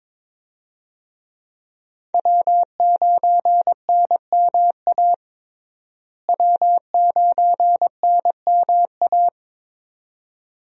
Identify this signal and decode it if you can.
Morse code, heard as a keyed tone, about 22 wpm, 700 Hz. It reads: W9NMA W9NMA
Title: USA Call Signs - 5 Characters - Repeated 2x - 22wpm